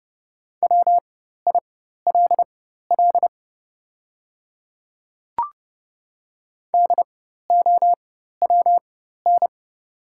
Morse code, heard as a keyed tone, about 30 wpm, spaced as for 14 wpm, 700 Hz. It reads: WILL DOWN
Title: 4x Character Spacing - Top 300 Words - 30wpm